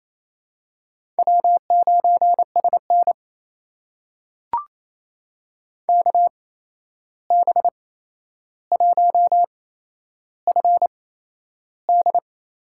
Morse code, heard as a keyed tone, about 28 wpm, spaced as for 11 wpm, 700 Hz. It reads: W9SN KB1FD
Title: USA Call Signs - 8x Character Spacing - 28wpm